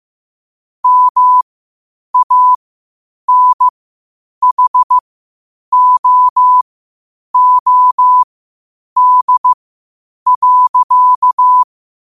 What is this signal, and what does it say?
Morse code, heard as a keyed tone, about 15 wpm, spaced as for 8 wpm, 1000 Hz. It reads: MANHOOD.
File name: qotdfiles / 8wpm